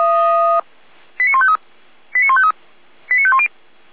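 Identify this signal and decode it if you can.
Five-tone selective call (ZVEI), heard as quick runs of five tones, tones of about 70 ms. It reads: 89153 89153 89510
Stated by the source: ZVEI five-tone selective call received by SDR.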